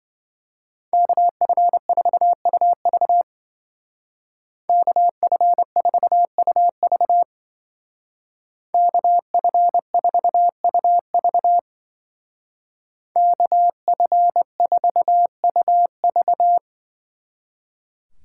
Morse code, heard as a keyed tone, about 25 wpm, 700 Hz. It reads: KF4UV KF4UV KF4UV KF4UV
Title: Speed Racing - USA 2x2 Call Signs 20wpm